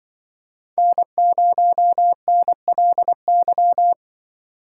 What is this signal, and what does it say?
Morse code, heard as a keyed tone, about 25 wpm, 700 Hz. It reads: N0NLY
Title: Speed Racing - USA 1x3 Call Signs 20wpm